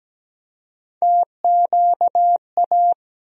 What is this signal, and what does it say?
Morse code, heard as a keyed tone, about 17 wpm, 700 Hz. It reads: TQA
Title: USA Call Sign Postfixes - 3 Letters 17wpm